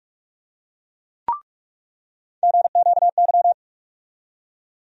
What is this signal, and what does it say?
Morse code, heard as a keyed tone, about 46 wpm, 700 Hz. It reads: GQY
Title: USA Call Sign Postfixes - 3 Letters 45wpm